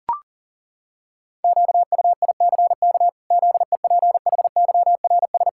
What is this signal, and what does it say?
Morse code, heard as a keyed tone, about 40 wpm, 700 Hz. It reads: QUICK ZEPHYRS
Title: Pangram Challenge v2 - 40wpm